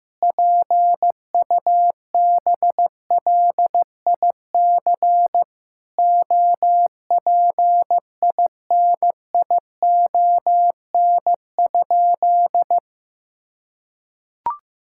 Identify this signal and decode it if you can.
Morse code, heard as a keyed tone, about 15 wpm, 700 Hz. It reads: PUBLIC OPINION?
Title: Sentences from Top 1600 Words 15wpm